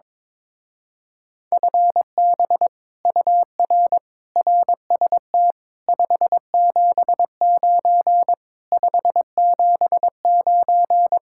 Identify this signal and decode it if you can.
Morse code, heard as a keyed tone, about 21 wpm, 700 Hz. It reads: FB UR RST 579 579
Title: Speed-Racing Common QSO Phrases - 22wpm